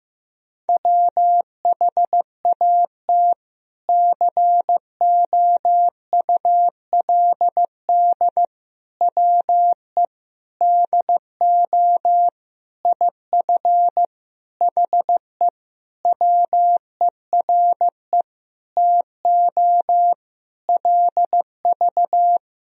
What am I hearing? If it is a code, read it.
Morse code, heard as a keyed tone, about 16 wpm, 700 Hz. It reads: WHAT COULD WE DO IF HE WERE TO LV
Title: Sentences from Top 400 Words and Abbreviations 15wpm